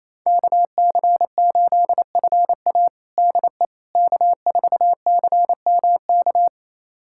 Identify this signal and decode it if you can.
Morse code, heard as a keyed tone, about 28 wpm, 700 Hz. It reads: KC8FA DE K4CMK
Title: USA Callsign DE Callsign - 5 Characters - 28wpm